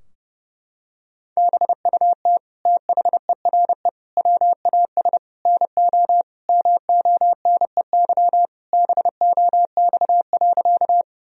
Morse code, heard as a keyed tone, about 30 wpm, 700 Hz. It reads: BUT THERE WAS NO MONEY BOX.
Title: Sentences from Top 1000 Words - Review Part 2 - 30wpm